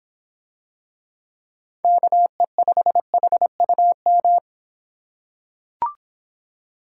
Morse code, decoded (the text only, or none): KE5HUM